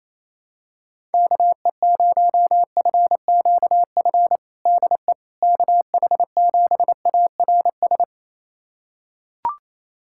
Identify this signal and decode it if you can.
Morse code, heard as a keyed tone, about 28 wpm, 700 Hz. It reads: KE0FQF DE KH7ARS
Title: USA Callsign DE Callsign - 6 Characters - 28wpm